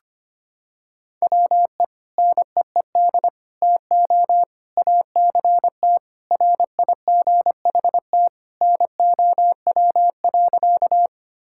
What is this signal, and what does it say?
Morse code, heard as a keyed tone, about 25 wpm, 700 Hz. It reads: WE NEED TO ACT RIGHT NOW.
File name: practice / 25wpm / Sentences with 2-Word Phrases - 25wpm